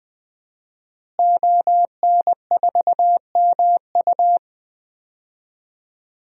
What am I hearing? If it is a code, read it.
Morse code, heard as a keyed tone, about 20 wpm, 700 Hz. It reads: ON4MU
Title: DXCC-100 Call Signs - 5 Characters - Repeated 3x - 20wpm